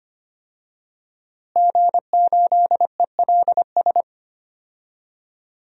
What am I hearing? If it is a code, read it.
Morse code, heard as a keyed tone, about 25 wpm, 700 Hz. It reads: G8ELS
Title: DXCC-100 Call Signs - 5 Characters - Repeated 2x - 25wpm